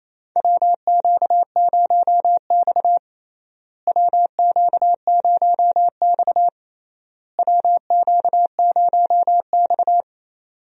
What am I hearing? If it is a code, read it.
Morse code, heard as a keyed tone, about 28 wpm, 700 Hz. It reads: WQ0X WQ0X WQ0X
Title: USA Call Signs - 4 Characters - Repeated 3x - 28wpm